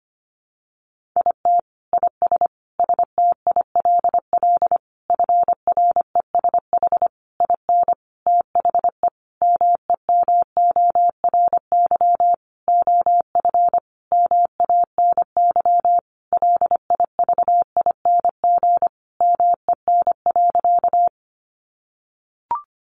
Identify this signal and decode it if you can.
Morse code, heard as a keyed tone, about 25 wpm, 700 Hz. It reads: IT IS STILL FRESH IN THE MEMORY OF MANY LIVING MEN.
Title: Sentences from Top 1300 Words 25wpm